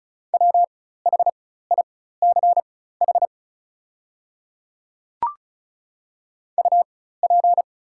Morse code, heard as a keyed tone, about 36 wpm, spaced as for 16 wpm, 690 Hz. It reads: WHICH UP
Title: 4x Character Spacing - Top 200 Words - 35wpm